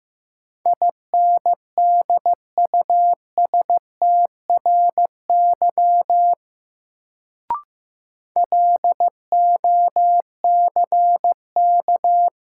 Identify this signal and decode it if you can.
Morse code, heard as a keyed tone, about 15 wpm, 700 Hz. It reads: INDUSTRY LOCK
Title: Sentences from Top 1300 Words - New Word Drill Alt - 15wpm